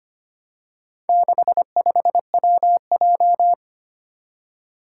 Morse code, heard as a keyed tone, about 25 wpm, 700 Hz. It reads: 65WJ
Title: Four Challenging Letter-Number Combinations 25wpm